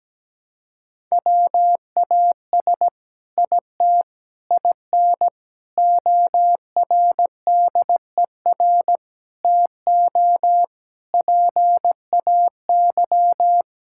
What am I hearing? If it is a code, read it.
Morse code, heard as a keyed tone, about 17 wpm, 700 Hz. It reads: WAS IT IN ORDER TO PAY
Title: Sentences from Top 1000 Words - Review Part 6 - 17wpm